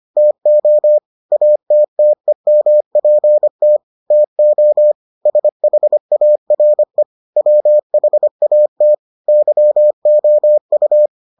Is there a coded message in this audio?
Yes: Morse code from 0.2 to 11.1 s, heard as a keyed tone, about 25 wpm, 600 Hz. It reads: TO ATTEMPT TO SHARE WHAT YOU